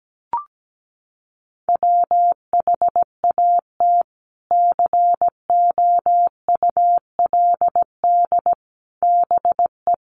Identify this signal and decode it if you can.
Morse code, heard as a keyed tone, about 17 wpm, 700 Hz. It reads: WHAT COULD BE
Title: Sentences from Top 400 Words 17wpm